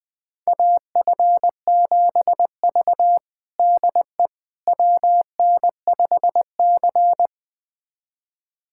Morse code, heard as a keyed tone, about 20 wpm, 700 Hz. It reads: AF7V DE WN5C
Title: USA Callsign DE Callsign - 4 Characters - 20wpm